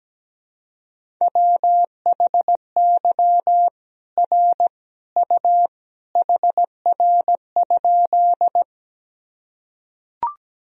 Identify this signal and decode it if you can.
Morse code, heard as a keyed tone, about 17 wpm, 700 Hz. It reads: WHY R U HR?
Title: Sentences from Top 200 Words and Abbreviations 17wpm